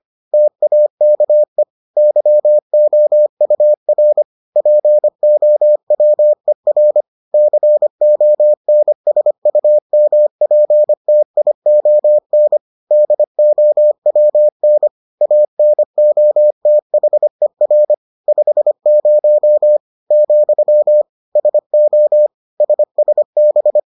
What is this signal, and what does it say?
Morse code, heard as a keyed tone, about 25 wpm, 600 Hz. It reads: TAKE YOUR POWER CONSUMPTION DOWN ANOTHER 50 , SO SSB